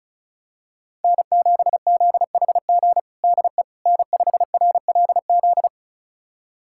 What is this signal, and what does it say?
Morse code, heard as a keyed tone, about 34 wpm, 700 Hz. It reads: N7ZHG DE N5RLZ
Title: USA Callsign DE Callsign - 5 Characters - 35wpm